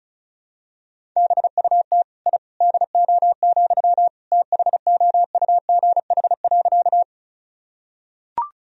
Morse code, heard as a keyed tone, about 36 wpm, 700 Hz. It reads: BUT I DO, THOUGH.
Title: Sentences from Top 1000 Words - Review Part 2 - 35wpm